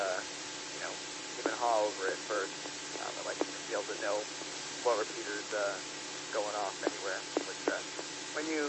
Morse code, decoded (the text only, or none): none